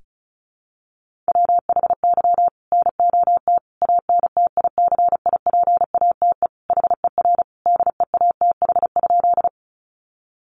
Morse code, heard as a keyed tone, about 34 wpm, 700 Hz. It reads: WHY NOT ANTICIPATE HER DEATH?
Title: Sentences from Top 1500 Words 35wpm